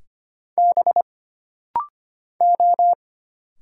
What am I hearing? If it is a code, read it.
Morse code, heard as a keyed tone, about 25 wpm, 700 Hz. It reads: B O
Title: Single Letters - Rapid-Fire 25wpm